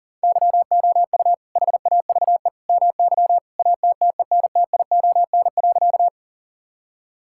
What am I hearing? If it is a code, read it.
Morse code, heard as a keyed tone, about 37 wpm, 700 Hz. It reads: YOU HAVE MY ATTENTION.
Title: Sentences from Top 1000 Words - Review Part 6 - 40wpm